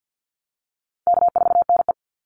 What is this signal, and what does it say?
Morse code, heard as a keyed tone, about 50 wpm, 700 Hz. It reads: K4NE
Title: Easy Call Signs - 50wpm